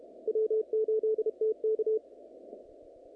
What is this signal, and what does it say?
Morse code, heard as a keyed tone, about 28 wpm, 440 Hz. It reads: W8TK
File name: contests / naqp-cw-k5tr-op-k5pi- -2200